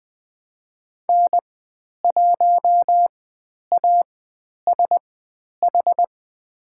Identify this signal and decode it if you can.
Morse code, heard as a keyed tone, about 20 wpm, spaced as for 10 wpm, 700 Hz. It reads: N1ASH
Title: 095 Lesson 16g - Letter c - Call Signs 20wpm